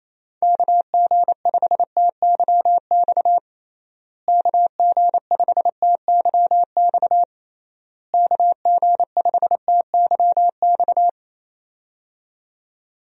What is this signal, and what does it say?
Morse code, heard as a keyed tone, about 27 wpm, 700 Hz. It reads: KG5TYX KG5TYX KG5TYX
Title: USA Call Signs - 6 Characters - Repeated 3x - 28wpm